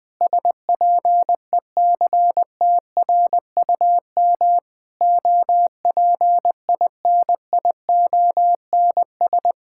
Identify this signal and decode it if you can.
Morse code, heard as a keyed tone, about 20 wpm, 700 Hz. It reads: SPECTRUM OPINIONS
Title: Sentences with 7-Word Phrases - Spoken Beginning - 20wpm